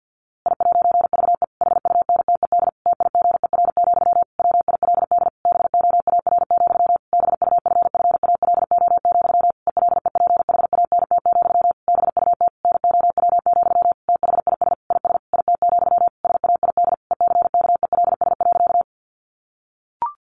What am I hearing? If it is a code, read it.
Morse code, heard as a keyed tone, about 50 wpm, 700 Hz. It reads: I'VE HUNTED TIGER, WILD BOAR, BUFFALO, ELEPHANT, BUT NOW, THIS IS IT, SAID ECKELS.
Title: A Sound Of Thunder 50wpm